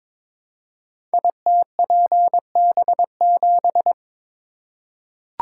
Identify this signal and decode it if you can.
Morse code, heard as a keyed tone, about 22 wpm, 700 Hz. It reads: ITPB7
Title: Code Groups - 5 Letter:Numbers - 22wpm